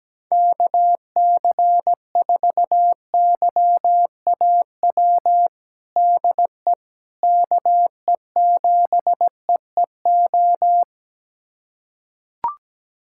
Morse code, heard as a keyed tone, about 17 wpm, 700 Hz. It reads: KC4YAW DE KE7EEO
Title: USA Callsign DE Callsign 17wpm